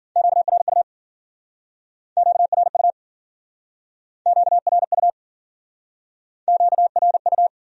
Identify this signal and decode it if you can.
Morse code, heard as a keyed tone, about 51 wpm, 700 Hz. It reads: QRU QRU QRU QRU